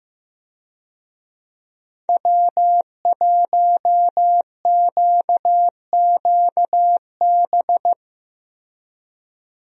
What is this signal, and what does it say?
Morse code, heard as a keyed tone, about 16 wpm, 700 Hz. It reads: W1QQB
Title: USA Call Signs - 5 Characters - Repeated 2x - 15wpm